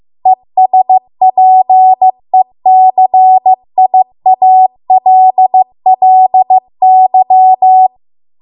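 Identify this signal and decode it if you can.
Morse code, heard as a keyed tone, about 15 wpm, 750 Hz. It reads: ESPECIALLY
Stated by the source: Morse code practice at 15 wpm from the national amateur radio association's code archive.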